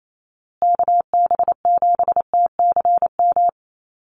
Morse code, heard as a keyed tone, about 28 wpm, 700 Hz. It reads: KB7TCM
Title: USA Call Signs Repeated 2x 28wpm